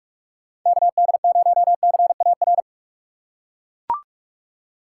Morse code, heard as a keyed tone, about 41 wpm, 700 Hz. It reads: KD0CAR
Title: USA Call Signs - 6 Characters - 2x Character Spacing - 45wpm